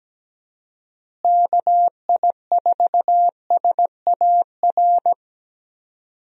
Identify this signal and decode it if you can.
Morse code, heard as a keyed tone, about 17 wpm, 700 Hz. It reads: KI4SAR
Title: USA Call Signs - 6 Characters - Repeated 3x - 17wpm